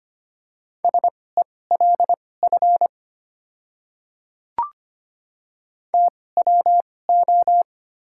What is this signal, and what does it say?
Morse code, heard as a keyed tone, about 25 wpm, spaced as for 18 wpm, 700 Hz. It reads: SELF TWO
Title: 2x Character Spacing - Top 300 Words - 25wpm